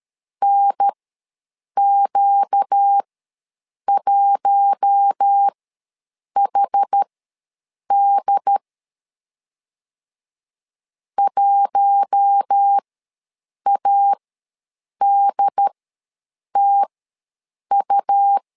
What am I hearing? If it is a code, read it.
Morse code, heard as a keyed tone, about 13 wpm, spaced as for 7 wpm, 800 Hz. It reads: NQ1HD 1ADTU
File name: LEZIONI / 4 Q1 ws 12wpm 2129